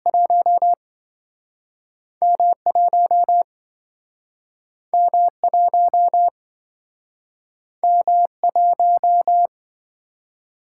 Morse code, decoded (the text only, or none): 1M 1M 1M 1